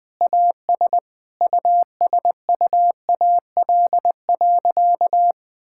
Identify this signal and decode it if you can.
Morse code, heard as a keyed tone, about 20 wpm, 700 Hz. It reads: AS USUAL.
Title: Sentences with 5-Word Phrases - Spoken Beginning - 20wpm